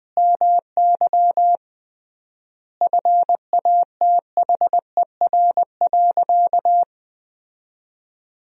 Morse code, heard as a keyed tone, about 20 wpm, 700 Hz. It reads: MY FATHER.
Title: 3x Word-Spacing - Sentences from Top 500 Words - 20wpm